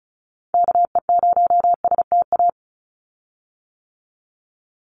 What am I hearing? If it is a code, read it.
Morse code, heard as a keyed tone, about 34 wpm, 700 Hz. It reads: KE0STA
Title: USA Call Signs 35wpm